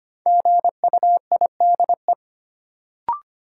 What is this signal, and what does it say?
Morse code, heard as a keyed tone, about 25 wpm, 700 Hz. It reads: GUIDE
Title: Sentences from Top 600 Words - New Word Drill - 25wpm